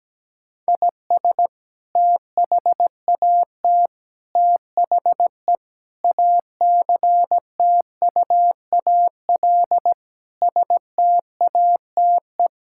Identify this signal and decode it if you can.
Morse code, heard as a keyed tone, about 17 wpm, 700 Hz. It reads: IS THAT THE ACTUAL STATE
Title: Sentences from Top 1000 Words - Review Part 5 - 17wpm